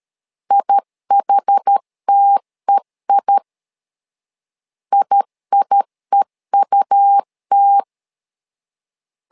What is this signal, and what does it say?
Morse code, heard as a keyed tone, about 13 wpm, 800 Hz. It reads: IHTEI IIEUT
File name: LEZIONI / 2 UHDN / 2 UHD 12wpm 1540